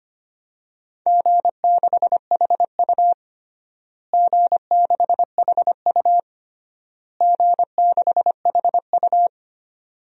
Morse code, heard as a keyed tone, about 25 wpm, 700 Hz. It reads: G6HU G6HU G6HU
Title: DXCC-100 Call Signs - 4 Characters - Repeated 3x - 25wpm